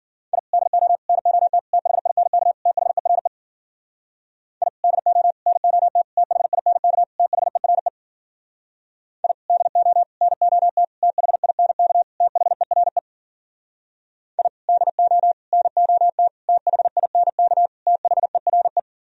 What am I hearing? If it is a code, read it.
Morse code, heard as a keyed tone, about 54 wpm, 700 Hz. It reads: I DO NOT THINK THERE I DO NOT THINK THERE I DO NOT THINK THERE I DO NOT THINK THERE